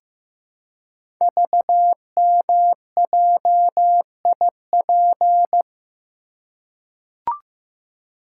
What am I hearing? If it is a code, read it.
Morse code, heard as a keyed tone, about 15 wpm, 700 Hz. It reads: VMJIP
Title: Code Groups - 5 Letter:Numbers - 15wpm